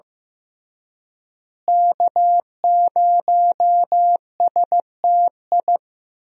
Morse code, heard as a keyed tone, about 15 wpm, 700 Hz. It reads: K0STI
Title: USA Call Signs - 5 Characters - Repeated 2x - 15wpm